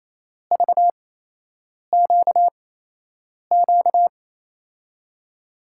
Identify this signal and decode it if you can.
Morse code, heard as a keyed tone, about 28 wpm, spaced as for 8 wpm, 700 Hz. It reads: VQQ
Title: USA Call Signs - 6 Characters - 8x Character Spacing - 28wpm